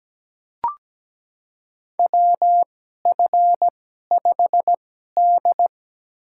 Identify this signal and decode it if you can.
Morse code, heard as a keyed tone, about 17 wpm, spaced as for 12 wpm, 700 Hz. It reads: WF5D
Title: USA Call Signs - 2x Character Spacing - 17wpm